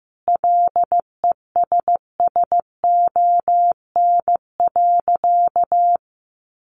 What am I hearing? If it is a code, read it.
Morse code, heard as a keyed tone, about 15 wpm, 700 Hz. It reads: LESSON.
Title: Sentences from Top 1200 Words 15wpm